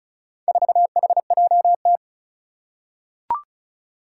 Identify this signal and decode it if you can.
Morse code, heard as a keyed tone, about 36 wpm, 700 Hz. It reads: 4HJT
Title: Four Challenging Letter-Number Combinations 35wpm